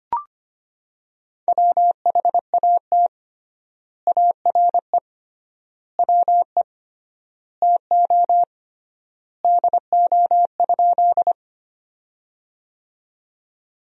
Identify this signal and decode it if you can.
Morse code, heard as a keyed tone, about 25 wpm, 700 Hz. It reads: WHAT ARE WE TO DO?
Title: 3x Word-Spacing - Sentences from Top 100 Words - 25wpm